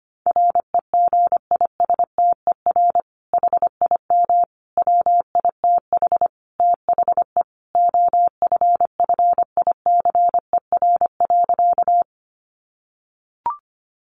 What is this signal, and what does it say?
Morse code, heard as a keyed tone, about 25 wpm, 700 Hz. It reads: REGISTER HIM WITH THE OFFICER.
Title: Sentences from Top 1200 Words 25wpm